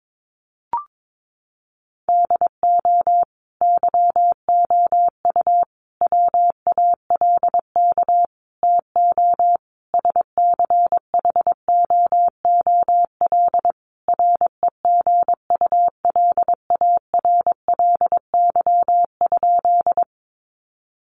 Morse code, heard as a keyed tone, about 22 wpm, 700 Hz. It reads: DO YOU WALK TO SCHOOL REGULARLY?
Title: Sentences from Top 1400 Words 22wpm